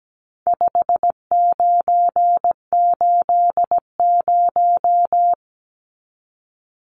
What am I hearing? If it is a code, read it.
Morse code, heard as a keyed tone, about 17 wpm, 700 Hz. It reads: 5980
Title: Numbers - 4-Digit 17wpm